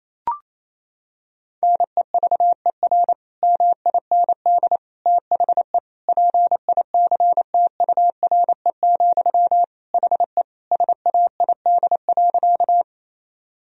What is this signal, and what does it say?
Morse code, heard as a keyed tone, about 28 wpm, 700 Hz. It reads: NEVER MIND THE PICTURE, HE SAID.
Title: Sentences from Top 1000 Words - Review Part 1 - 28wpm